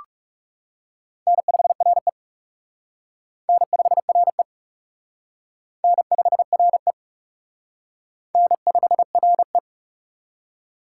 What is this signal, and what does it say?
Morse code, heard as a keyed tone, about 35 wpm, 700 Hz. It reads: N5RE N5RE N5RE N5RE
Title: Speed Racing - USA 1x2 Call Signs 30wpm